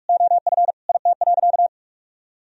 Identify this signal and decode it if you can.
Morse code, heard as a keyed tone, about 46 wpm, 700 Hz. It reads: OF IT.
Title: Sentences from Top 1000 Words - Review Part 3 - 45wpm